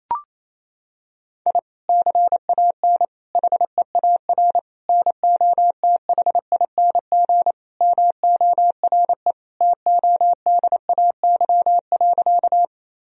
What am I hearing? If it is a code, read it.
Morse code, heard as a keyed tone, about 28 wpm, 700 Hz. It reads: I CAN HEAR NOTHING MORE TODAY.
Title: Sentences from Top 1000 Words - Review Part 4 - 28wpm